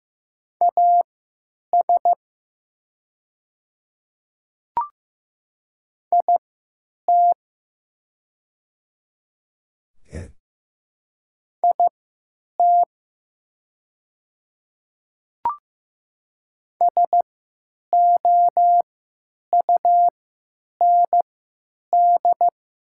Morse code, heard as a keyed tone, about 15 wpm, spaced as for 9 wpm, 700 Hz. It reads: AS IT IT SOUND
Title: 3x Character Spacing - Top 200 Words - 15wpm